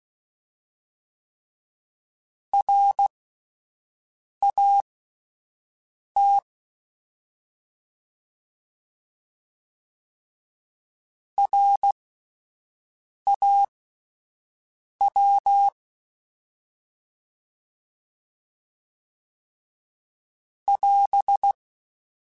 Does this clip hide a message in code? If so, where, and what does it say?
Yes: Morse code from 2.5 to 21.5 s, heard as a keyed tone, about 16 wpm, spaced as for 5 wpm, 800 Hz. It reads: RAT RAW <AS>